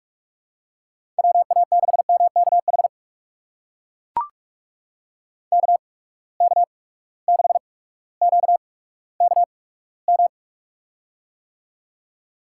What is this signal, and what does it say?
Morse code, heard as a keyed tone, about 41 wpm, spaced as for 19 wpm, 700 Hz. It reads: WA6MKH KK6QKM
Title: USA Call Signs - 6 Characters - 8x Character Spacing - 45wpm